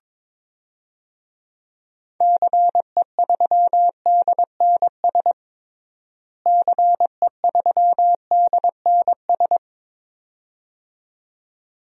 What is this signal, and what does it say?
Morse code, heard as a keyed tone, about 22 wpm, 700 Hz. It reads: CE3DNS CE3DNS
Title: DXCC-100 Call Signs - 6 Characters - Repeated 2x - 22wpm